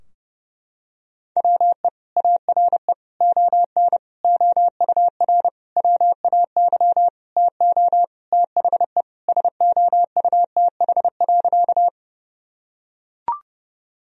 Morse code, heard as a keyed tone, about 32 wpm, 700 Hz. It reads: WE ARE ON OUR WAY TO THE SOUTH.
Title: Sentences from Top 1000 Words - Review Part 2 - 30wpm